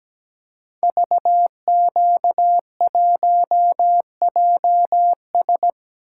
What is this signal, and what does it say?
Morse code, heard as a keyed tone, about 17 wpm, 700 Hz. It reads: VQ1JS
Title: Code Groups - 5 Letter:Numbers - 17wpm